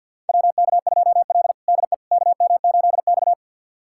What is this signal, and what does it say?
Morse code, heard as a keyed tone, about 45 wpm, 690 Hz. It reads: WK2L DE KM8X
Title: USA Callsign DE Callsign - 4 Characters - 50wpm